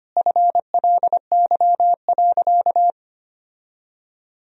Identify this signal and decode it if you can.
Morse code, heard as a keyed tone, about 25 wpm, 700 Hz. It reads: FLY.
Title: Sentences from Top 1000 Words - Review Part 2 - 25wpm